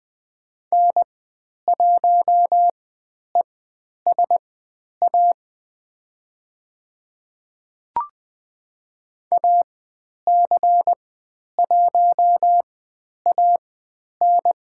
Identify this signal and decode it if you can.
Morse code, heard as a keyed tone, about 20 wpm, spaced as for 10 wpm, 700 Hz. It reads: N1ESA AC1AN
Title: 109 Lesson 18g - Letter m - Call Signs 20wpm